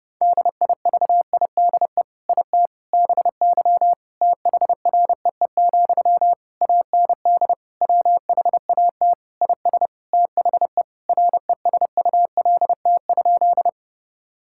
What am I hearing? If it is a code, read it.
Morse code, heard as a keyed tone, about 32 wpm, 700 Hz. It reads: DIVIDE IT BY THREE, AND WHAT IS THE RESULT?
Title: Sentences from Top 1600 Words 30wpm